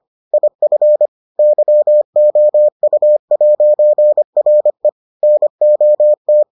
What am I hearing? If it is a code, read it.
Morse code, heard as a keyed tone, about 25 wpm, 600 Hz. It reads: IF YOU'RE NOT